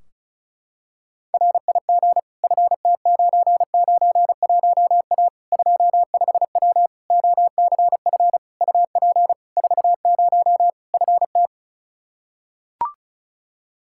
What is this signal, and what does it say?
Morse code, heard as a keyed tone, about 36 wpm, 700 Hz. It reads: RIG FT991A 25W OCF UP 40 FT
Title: Practice QSO - Exchange 2 - Info - 35wpm